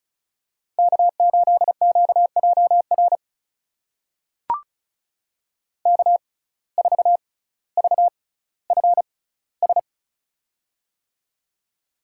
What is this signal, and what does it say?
Morse code, heard as a keyed tone, about 36 wpm, spaced as for 18 wpm, 700 Hz. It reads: K8QJR K4VFS
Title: USA Call Signs - 5 Characters - 6x Character Spacing - 35wpm